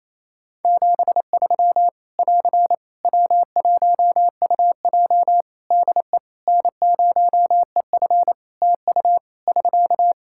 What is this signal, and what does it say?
Morse code, heard as a keyed tone, about 28 wpm, 700 Hz. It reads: 73 <AR> W1UJ DE N0EF TU <SK>
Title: Practice QSO - Exchange 4 - Ending - 28wpm